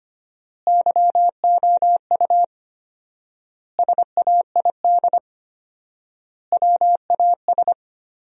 Morse code, heard as a keyed tone, about 25 wpm, 700 Hz. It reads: YOU SAID WAS
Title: 4x Word-Spacing - Sentences from Top 400 Words - 25wpm